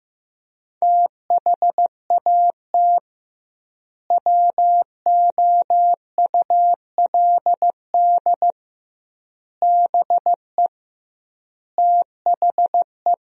Morse code, heard as a keyed tone, about 16 wpm, 700 Hz. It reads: THAT WOULD BE THE
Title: 2x Word-Spacing - Sentences from Top 400 Words - 15wpm